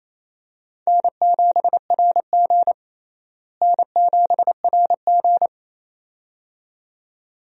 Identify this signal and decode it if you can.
Morse code, heard as a keyed tone, about 27 wpm, 700 Hz. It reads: N7RG N7RG
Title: USA Call Signs - 4 Characters - Repeated 2x - 28wpm